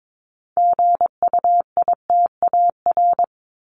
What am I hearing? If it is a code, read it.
Morse code, heard as a keyed tone, about 22 wpm, 700 Hz. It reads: GUITAR
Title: Top 1501-1750 Words 22wpm